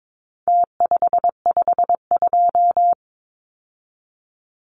Morse code, heard as a keyed tone, about 22 wpm, 700 Hz. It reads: T552
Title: Numbers - Contest Style - 4-Digit 22wpm